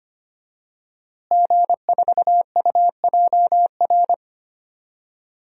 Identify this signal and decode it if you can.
Morse code, heard as a keyed tone, about 25 wpm, 700 Hz. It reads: G4UJR
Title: DXCC-100 Call Signs - 5 Characters - Repeated 2x - 25wpm